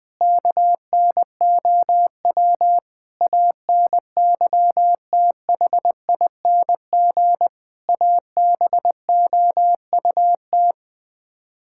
Morse code, heard as a keyed tone, about 21 wpm, 700 Hz. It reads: KNOW ANYTHING ABOUT
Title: Build A Phrase - 5-Word Phrases - 20wpm